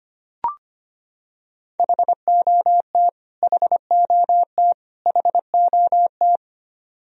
Morse code, heard as a keyed tone, about 25 wpm, 700 Hz. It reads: HOT HOT HOT
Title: Top 100 Words Repeated 25wpm